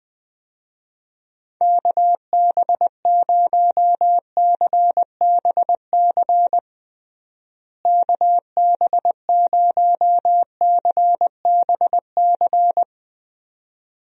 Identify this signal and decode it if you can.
Morse code, heard as a keyed tone, about 20 wpm, 700 Hz. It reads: KB0CBC KB0CBC
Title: USA Call Signs - 6 Characters - Repeated 2x - 20wpm